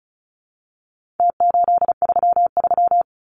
Morse code, heard as a keyed tone, about 34 wpm, 700 Hz. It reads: T833
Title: Numbers - Contest Style - 4-Digit 35wpm